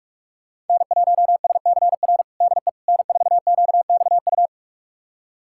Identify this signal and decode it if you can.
Morse code, heard as a keyed tone, about 45 wpm, 700 Hz. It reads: N1SCR DE N4QXU
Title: USA Callsign DE Callsign - 5 Characters - 45wpm